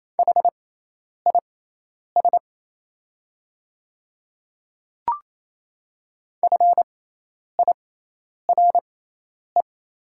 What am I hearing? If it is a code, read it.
Morse code, heard as a keyed tone, about 28 wpm, spaced as for 10 wpm, 690 Hz. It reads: HIS FIRE
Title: 6x Character Spacing - Top 500 Words - 28wpm